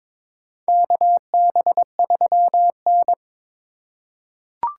Morse code, heard as a keyed tone, about 21 wpm, 700 Hz. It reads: KB3N
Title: USA Call Signs - 4 Characters - 2x Character Spacing - 22wpm